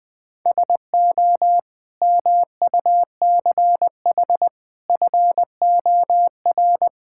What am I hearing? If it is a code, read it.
Morse code, heard as a keyed tone, about 20 wpm, 690 Hz. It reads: SO MUCH FOR
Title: Build A Phrase - 5-Word Phrases - 20wpm